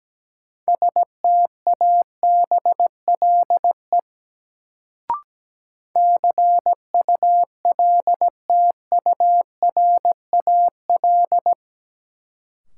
Morse code, heard as a keyed tone, about 17 wpm, 700 Hz. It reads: STABLE CULTURAL
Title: Sentences from Top 1500 Words - New Word Drill Alt - 17wpm